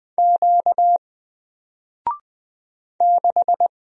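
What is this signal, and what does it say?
Morse code, heard as a keyed tone, about 20 wpm, 700 Hz. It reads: Q 6
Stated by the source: Single letter and number Morse drill at 20 wpm, from N4LCD.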